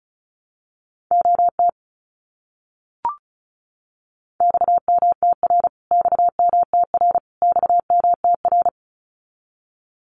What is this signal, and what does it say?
Morse code, heard as a keyed tone, about 34 wpm, 700 Hz. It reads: OT XMTR XMTR XMTR
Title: Abbreviations Repeated 3x - 35wpm